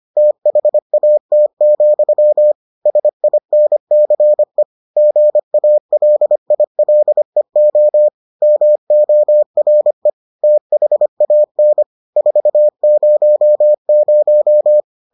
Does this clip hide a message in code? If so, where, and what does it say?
Yes: Morse code from 0.2 to 14.8 s, heard as a keyed tone, about 25 wpm, 600 Hz. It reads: THAT, SINCE GALILEO MORE THAN 400